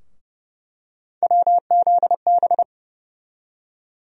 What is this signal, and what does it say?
Morse code, heard as a keyed tone, about 30 wpm, 700 Hz. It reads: WZB
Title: USA Call Sign Postfixes - 3 Letters 30wpm